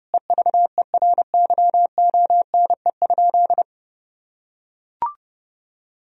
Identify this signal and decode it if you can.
Morse code, heard as a keyed tone, about 30 wpm, 700 Hz. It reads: EVERYONE?
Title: Sentences from Top 1000 Words - Review Part 4 - 30wpm